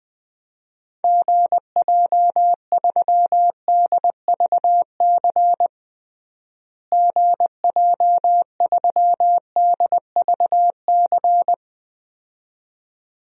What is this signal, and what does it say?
Morse code, heard as a keyed tone, about 21 wpm, 700 Hz. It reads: GJ3DVC GJ3DVC
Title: DXCC-100 Call Signs - 6 Characters - Repeated 2x - 20wpm